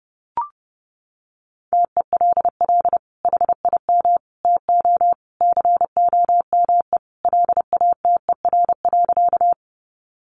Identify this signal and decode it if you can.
Morse code, heard as a keyed tone, about 30 wpm, 700 Hz. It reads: TELL HIM TO COME LATER.
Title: Sentences from Top 700 Words 30wpm